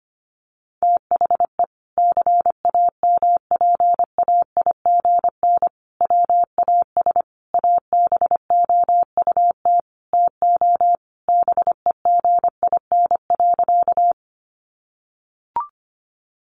Morse code, heard as a keyed tone, about 25 wpm, 700 Hz. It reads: THE CAMPAIGN WAS ABOUT TO BEGIN.
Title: Sentences from Top 1200 Words 25wpm